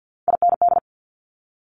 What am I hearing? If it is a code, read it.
Morse code, heard as a keyed tone, about 50 wpm, 700 Hz. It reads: IND
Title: Multi-letter Phonemes Quick Fire 50wpm